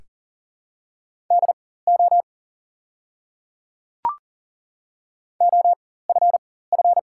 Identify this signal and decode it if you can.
Morse code, heard as a keyed tone, about 40 wpm, spaced as for 19 wpm, 700 Hz. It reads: DO OFF
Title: 4x Character Spacing - Top 200 Words - 40wpm